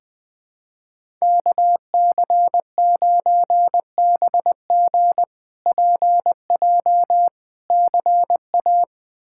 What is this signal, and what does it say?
Morse code, heard as a keyed tone, about 20 wpm, 700 Hz. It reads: KC9BG PJ CA
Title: K1USN SST QSO Practice - 20wpm